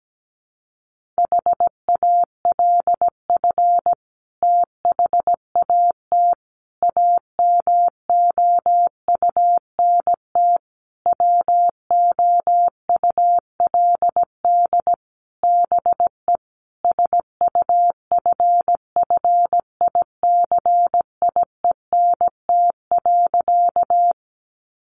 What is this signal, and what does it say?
Morse code, heard as a keyed tone, about 17 wpm, 700 Hz. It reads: HALF THAT AMOUNT WOULD BE SUFFICIENT.
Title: Sentences from Top 1400 Words 17wpm